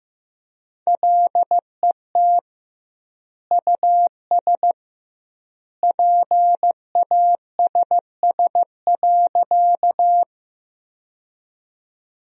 Morse code, heard as a keyed tone, about 15 wpm, 700 Hz. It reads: LET US PASS.
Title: 2x Word-Spacing - Sentences from Top 400 Words - 15wpm